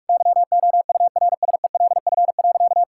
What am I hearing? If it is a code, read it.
Morse code, heard as a keyed tone, about 46 wpm, 700 Hz. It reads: YOURSELF.
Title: Sentences from Top 1600 Words 45wpm